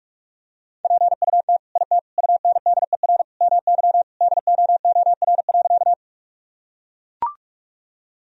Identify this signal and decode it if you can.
Morse code, heard as a keyed tone, about 41 wpm, 700 Hz. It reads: PUT IT UNDER MY DOOR.
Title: Sentences from Top 1000 Words - Review Part 1 - 45wpm